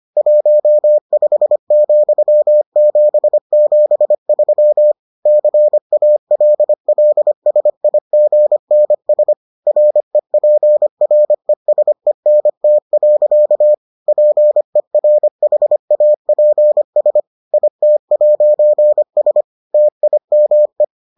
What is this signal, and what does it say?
Morse code, heard as a keyed tone, about 25 wpm, 600 Hz. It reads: 15,773 CALLSIGNS REPRESENT. PERHAPS IT'S TIME